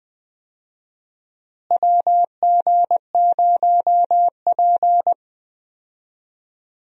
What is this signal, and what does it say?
Morse code, heard as a keyed tone, about 20 wpm, 700 Hz. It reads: WG0P
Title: USA Call Signs - 4 Characters - Repeated 3x - 20wpm